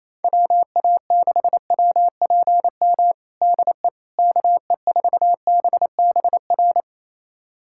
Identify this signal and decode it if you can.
Morse code, heard as a keyed tone, about 27 wpm, 700 Hz. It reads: WA6WPM DE KE4BBR
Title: USA Callsign DE Callsign - 6 Characters - 28wpm